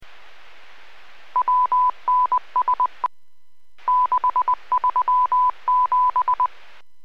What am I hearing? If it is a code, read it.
Morse code, heard as a keyed tone, about 20 wpm, 1000 Hz. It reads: WNS 637